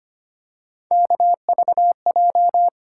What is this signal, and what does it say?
Morse code, heard as a keyed tone, about 25 wpm, 700 Hz. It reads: KVJ
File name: Three Letter-Number Combinations - 25wpm